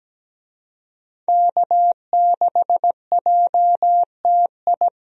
Morse code, heard as a keyed tone, about 17 wpm, 700 Hz. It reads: K6JTI
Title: USA Call Signs - 5 Characters - Repeated 3x - 17wpm